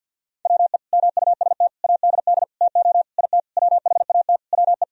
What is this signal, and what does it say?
Morse code, heard as a keyed tone, about 52 wpm, 700 Hz. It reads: WE MUST ADD TO IT WHAT WE
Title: Sentences from Top 1000 Words - Review Part 1 - 50wpm